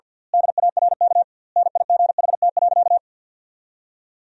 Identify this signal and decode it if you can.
Morse code, heard as a keyed tone, about 51 wpm, 700 Hz. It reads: DARK NIGHT.
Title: 2x Word-Spacing - Sentences from Top 500 Words - 50wpm